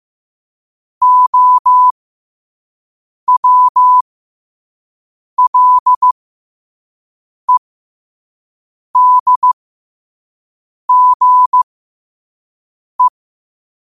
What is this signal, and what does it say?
Morse code, heard as a keyed tone, about 15 wpm, spaced as for 5 wpm, 1000 Hz. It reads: OWLEDGE